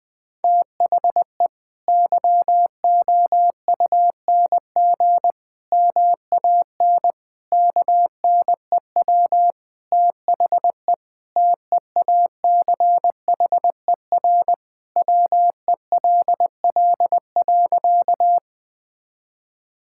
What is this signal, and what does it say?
Morse code, heard as a keyed tone, about 21 wpm, 700 Hz. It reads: THE YOUNG MAN KNEW THE TEACHER WELL.
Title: Sentences from Top 1000 Words - Review Part 6 - 20wpm